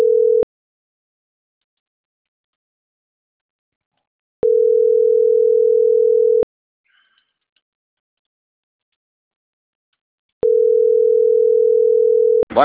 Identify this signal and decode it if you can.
Ringback tone (North America), 440 + 480 Hz.